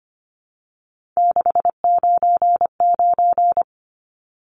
Morse code, heard as a keyed tone, about 25 wpm, 700 Hz. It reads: 699
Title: Numbers - 3-Digit 25wpm